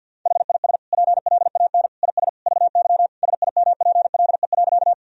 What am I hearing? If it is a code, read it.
Morse code, heard as a keyed tone, about 50 wpm, 700 Hz. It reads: HIS PLAN IS VY SIMPLE.
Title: Sentences from Top 500 Words and Abbreviations 50wpm